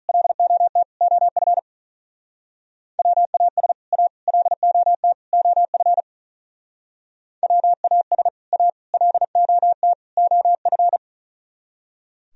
Morse code, heard as a keyed tone, about 45 wpm, 700 Hz. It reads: LOT OF WAS A LOT OF WAS A LOT OF